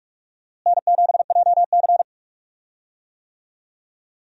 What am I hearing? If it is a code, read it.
Morse code, heard as a keyed tone, about 47 wpm, 700 Hz. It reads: N7JC
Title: USA Call Signs - 4 Characters - Repeated 2x - 45wpm